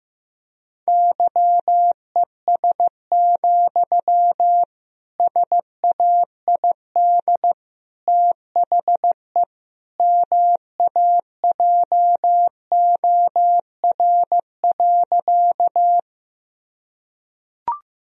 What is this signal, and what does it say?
Morse code, heard as a keyed tone, about 16 wpm, 700 Hz. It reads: YES, SAID THE MAJOR.
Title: Sentences from Top 1000 Words - Review Part 4 - 15wpm